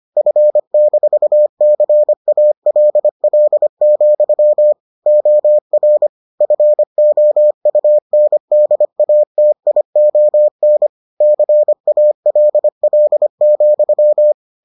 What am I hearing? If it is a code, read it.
Morse code, heard as a keyed tone, about 25 wpm, 600 Hz. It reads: F-CALL, OR FOUNDATION CALL,